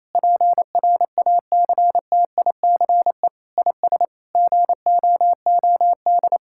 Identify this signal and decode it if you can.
Morse code, heard as a keyed tone, about 28 wpm, 700 Hz. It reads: PRACTICE IS GOOD
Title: Sentences from Top 1000 Words - Review Part 5 - 28wpm